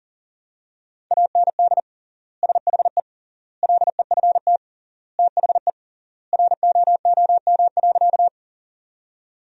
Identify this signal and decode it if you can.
Morse code, heard as a keyed tone, about 40 wpm, 700 Hz. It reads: AND SHE LEFT THE ROOM.
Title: 3x Word-Spacing - Sentences from Top 300 Words - 40wpm